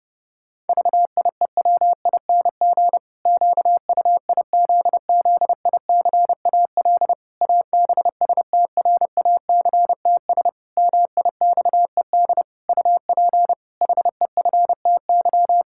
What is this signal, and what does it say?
Morse code, heard as a keyed tone, about 32 wpm, 700 Hz. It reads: VIEWING QUIZZICAL ABSTRACTS MIXED UP HEFTY
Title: Pangram Challenge v2 - 30wpm